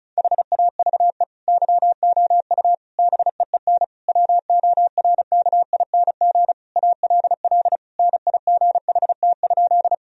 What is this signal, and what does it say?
Morse code, heard as a keyed tone, about 34 wpm, 700 Hz. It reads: HAVE YOU BEEN WORKING ALL NIGHT?
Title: Sentences from Top 1000 Words - Review Part 4 - 35wpm